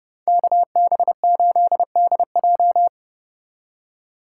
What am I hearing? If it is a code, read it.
Morse code, heard as a keyed tone, about 34 wpm, 700 Hz. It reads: KB8DJ